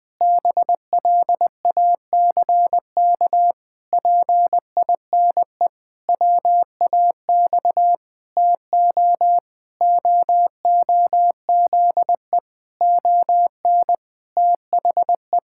Morse code, heard as a keyed tone, about 20 wpm, 700 Hz. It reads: BLACK PINE WAX TO OOZE ON THE
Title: Pangram Challenge v2 - 20wpm